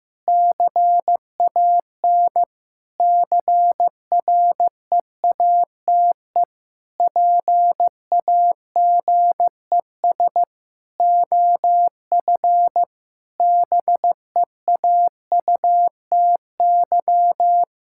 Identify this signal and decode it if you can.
Morse code, heard as a keyed tone, about 15 wpm, 700 Hz. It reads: CAN CREATE PAGES OF BEAUTY
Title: Pangram Challenge v2 - 15wpm